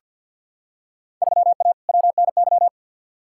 Morse code, heard as a keyed tone, about 50 wpm, 700 Hz. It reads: 3A WNY
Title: Field Day Exchanges - 50wpm